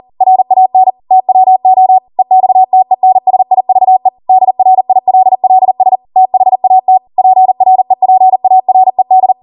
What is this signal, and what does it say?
Morse code, heard as a keyed tone, about 40 wpm, 750 Hz. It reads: RAN TWO EXTENSIVE DRILLS THAT PREPARED